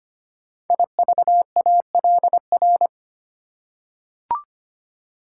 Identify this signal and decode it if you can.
Morse code, heard as a keyed tone, about 25 wpm, 700 Hz. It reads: IVALR
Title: Code Groups - 5 Letters - 25wpm